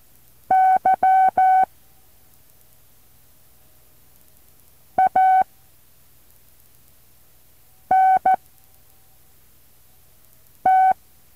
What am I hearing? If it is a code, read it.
Morse code, heard as a keyed tone, about 14 wpm, spaced as for 3 wpm, 760 Hz. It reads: Y ANT
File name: intl morse code2